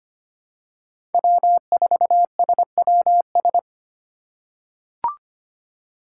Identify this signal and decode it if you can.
Morse code, heard as a keyed tone, about 25 wpm, 700 Hz. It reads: W4SWS
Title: USA Call Signs - 5 Characters - 4x Character Spacing - 25wpm